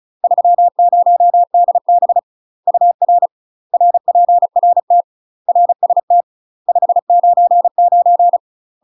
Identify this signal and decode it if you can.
Morse code, heard as a keyed tone, about 35 wpm, 700 Hz. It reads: 30DB UR RPRT RST 599
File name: UK MASTER QSOs TRAINING / 35WPM / UK MASTER QSOs 0017